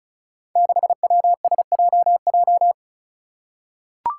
Morse code, heard as a keyed tone, about 34 wpm, 700 Hz. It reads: 6WSJJ
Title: Five Challenging Letter-Number Combinations 35wpm